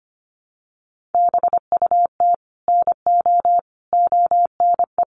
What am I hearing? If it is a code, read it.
Morse code, heard as a keyed tone, about 25 wpm, 700 Hz. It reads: BUT NO ONE
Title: Sentences from Top 1000 Words 25wpm